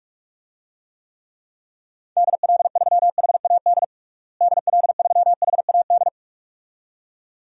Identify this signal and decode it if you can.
Morse code, heard as a keyed tone, about 46 wpm, 700 Hz. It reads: DL3HAD DL3HAD
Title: DXCC-100 Call Signs - 6 Characters - Repeated 2x - 45wpm